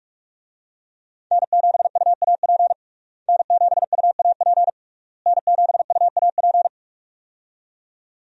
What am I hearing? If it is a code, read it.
Morse code, heard as a keyed tone, about 46 wpm, 700 Hz. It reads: N7UAP N7UAP N7UAP
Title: USA Call Signs - 5 Characters - Repeated 3x - 45wpm